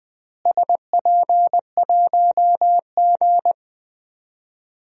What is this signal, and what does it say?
Morse code, heard as a keyed tone, about 21 wpm, 700 Hz. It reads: SP1G
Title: DXCC-100 Call Signs - 4 Characters - Repeated 2x - 20wpm